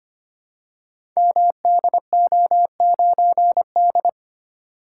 Morse code, heard as a keyed tone, about 25 wpm, 700 Hz. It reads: MDO9D
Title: Code Groups - 5 Letter:Numbers - 25wpm